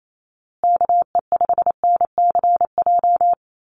Morse code, heard as a keyed tone, about 28 wpm, 700 Hz. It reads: KE5NCJ
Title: USA Call Signs Repeated 3x 28wpm